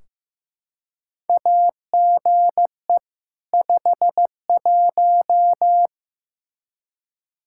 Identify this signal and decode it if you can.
Morse code, heard as a keyed tone, about 15 wpm, 700 Hz. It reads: AGE 51
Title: Practice QSO - Exchange 3 - More Info - 15wpm